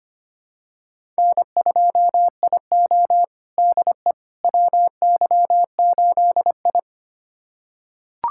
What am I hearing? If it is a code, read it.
Morse code, heard as a keyed tone, about 25 wpm, 700 Hz. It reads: N2IO DE WY8I
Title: USA Callsign DE Callsign - 4 Characters - 25wpm